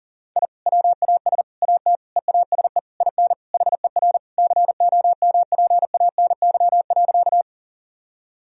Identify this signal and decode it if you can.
Morse code, heard as a keyed tone, about 40 wpm, 700 Hz. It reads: I WAS AT EASE IN HER COMPANY.
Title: Sentences from Top 1000 Words - Review Part 1 - 40wpm